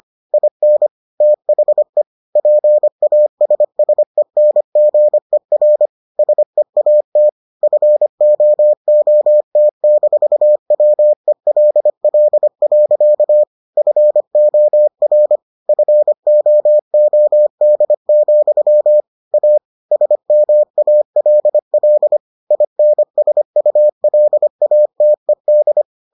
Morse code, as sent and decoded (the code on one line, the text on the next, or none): .. -. / - .... / .--. .- ... ... -. --. .-. / ... .- - / ..-. --- --- - -....- .-- . .-.. .-.. .-.-.- / ..-. --- .-. / ..-. --- --- -.. --..-- / .- / ... -- .- .-.. .-.. / .. -. ... ..- .-.. .- - . -..
IN THE PASSENGER SEAT FOOT-WELL. FOR FOOD, A SMALL INSULATED